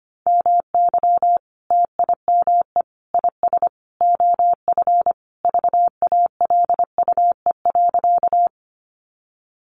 Morse code, heard as a keyed tone, about 25 wpm, 700 Hz. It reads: MY TIME IS OF VALUE.
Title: Sentences from Top 600 Words 25wpm